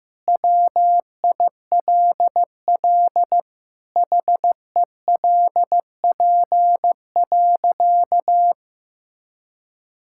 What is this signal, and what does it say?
Morse code, heard as a keyed tone, about 16 wpm, 700 Hz. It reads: WILL HELP.
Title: Sentences from Top 1000 Words - Review Part 5 - 15wpm